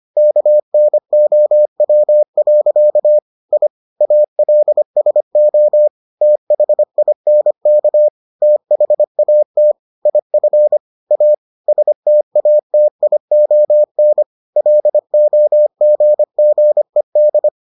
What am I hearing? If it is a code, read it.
Morse code, heard as a keyed tone, about 25 wpm, 600 Hz. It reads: KNOW. I ALSO THINK THAT IF A STATION LOGGED